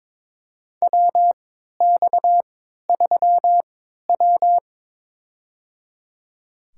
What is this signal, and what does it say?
Morse code, heard as a keyed tone, about 23 wpm, spaced as for 13 wpm, 700 Hz. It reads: WX3W
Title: USA Call Signs - 4 Characters - 3x Character Spacing - 22wpm